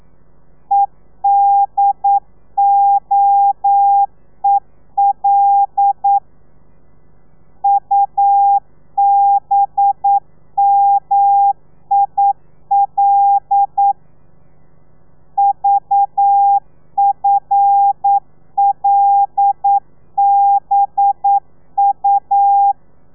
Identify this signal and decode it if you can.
Morse code, heard as a keyed tone, about 9 wpm, 800 Hz. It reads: EDOEL UBMIL VFLBU
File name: telegrafia / i2tfj / Lez5 40